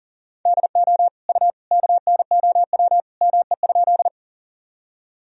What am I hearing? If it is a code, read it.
Morse code, heard as a keyed tone, about 40 wpm, 700 Hz. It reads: DO U KNOW ME?
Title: Sentences from Top 200 Words and Abbreviations 40wpm